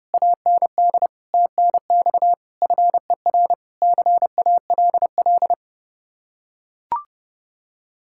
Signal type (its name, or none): Morse code